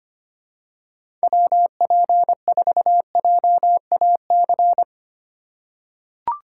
Morse code, heard as a keyed tone, about 25 wpm, 700 Hz. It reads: WP4JAC
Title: USA Call Signs - 6 Characters - 2x Character Spacing - 25wpm